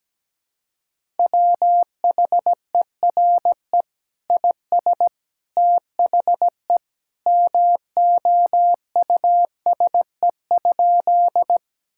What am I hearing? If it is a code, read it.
Morse code, heard as a keyed tone, about 17 wpm, 700 Hz. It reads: WHERE IS THE MOUSE?
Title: Sentences from Top 1000 Words - Review Part 5 - 17wpm